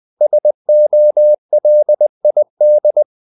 Morse code, heard as a keyed tone, about 20 wpm, 600 Hz. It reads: SOLID
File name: the old man and the sea part 2 20wpm